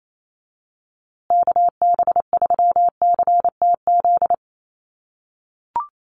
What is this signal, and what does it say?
Morse code, heard as a keyed tone, about 28 wpm, 700 Hz. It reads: KB3CTZ
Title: USA Call Signs Repeated 3x 28wpm